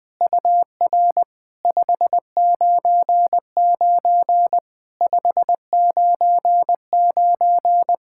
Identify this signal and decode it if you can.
Morse code, heard as a keyed tone, about 20 wpm, 700 Hz. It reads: UR 599 599
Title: POTA QSO Practice - 20wpm